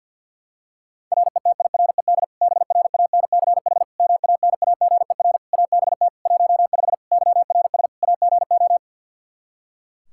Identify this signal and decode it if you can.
Morse code, heard as a keyed tone, about 48 wpm, 700 Hz. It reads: RETIRED BRANCH MANAGER ABT 15 YRS AGO
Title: Practice QSO - Exchange 3 - More Info - 50wpm